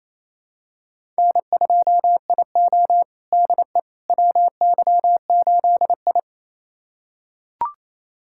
Morse code, heard as a keyed tone, about 28 wpm, 700 Hz. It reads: N2IO DE WY8I
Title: USA Callsign DE Callsign - 4 Characters - 28wpm